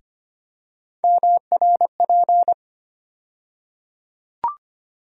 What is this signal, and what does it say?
Morse code, heard as a keyed tone, about 25 wpm, 700 Hz. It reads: MRP
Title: USA Call Sign Postfixes - 3 Letters 25wpm